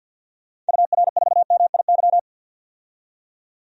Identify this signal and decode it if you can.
Morse code, heard as a keyed tone, about 53 wpm, 700 Hz. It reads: UR4MIY
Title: DXCC-100 Call Signs - 6 Characters - 50wpm